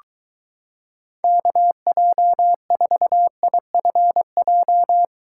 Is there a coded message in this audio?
Yes: Morse code from 1.2 to 5.0 s, heard as a keyed tone, about 23 wpm, 700 Hz. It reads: KJ4IFJ